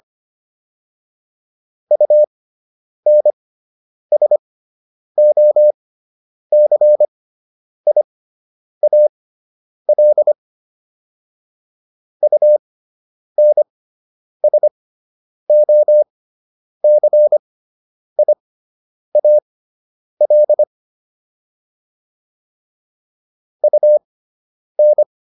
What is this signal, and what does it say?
Morse code, heard as a keyed tone, about 25 wpm, spaced as for 9 wpm, 600 Hz. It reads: UNSOCIAL UNSOCIAL UN